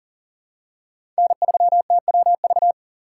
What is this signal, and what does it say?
Morse code, heard as a keyed tone, about 40 wpm, 700 Hz. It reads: N3TWV
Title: USA Call Signs - 5 Characters - 2x Character Spacing - 40wpm